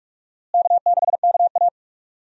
Morse code, heard as a keyed tone, about 42 wpm, 700 Hz. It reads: K6KA
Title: USA Call Signs - 4 Characters - Repeated 2x - 45wpm